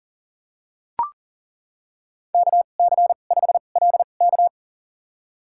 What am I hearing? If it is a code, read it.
Morse code, heard as a keyed tone, about 40 wpm, spaced as for 29 wpm, 700 Hz. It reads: KC5LK
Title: USA Call Signs - 5 Characters - 2x Character Spacing - 40wpm